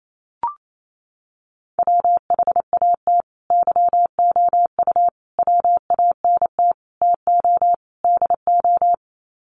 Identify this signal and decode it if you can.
Morse code, heard as a keyed tone, about 28 wpm, 700 Hz. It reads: WHAT YOU WANT TO DO
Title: Sets of 5 Words Encore 28wpm